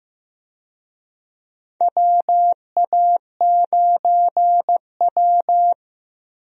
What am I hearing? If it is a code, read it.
Morse code, heard as a keyed tone, about 15 wpm, 700 Hz. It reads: WA9W
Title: USA Call Signs - 4 Characters - Repeated 3x - 15wpm